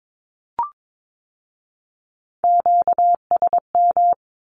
Morse code, heard as a keyed tone, about 22 wpm, 700 Hz. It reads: QSM
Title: Q-Codes with Spoken Meaning - 22wpm